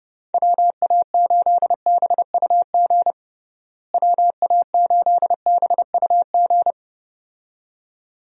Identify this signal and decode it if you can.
Morse code, heard as a keyed tone, about 29 wpm, 700 Hz. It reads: WA8BUG WA8BUG
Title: USA Call Signs - 6 Characters - Repeated 2x - 30wpm